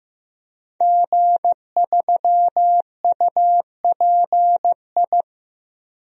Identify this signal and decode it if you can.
Morse code, heard as a keyed tone, about 15 wpm, 700 Hz. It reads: G3UPI
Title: DXCC-100 Call Signs - 5 Characters - 15wpm